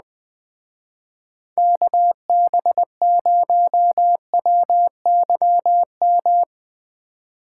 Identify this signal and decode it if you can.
Morse code, heard as a keyed tone, about 20 wpm, 700 Hz. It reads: KB0WYM